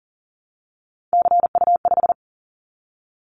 Morse code, heard as a keyed tone, about 38 wpm, 700 Hz. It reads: CU5
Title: Callsign Prefixes for Top 50 DXCC - 40wpm